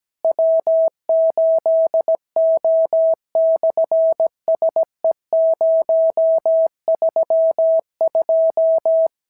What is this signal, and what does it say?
Morse code, heard as a keyed tone, about 17 wpm, 650 Hz. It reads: W8O/SE032
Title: SOTA QSO Practice 17wpm